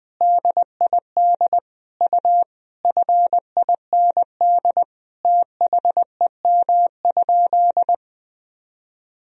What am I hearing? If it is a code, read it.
Morse code, heard as a keyed tone, about 20 wpm, 700 Hz. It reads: DID U FIND THEM?
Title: Sentences from Top 200 Words and Abbreviations 20wpm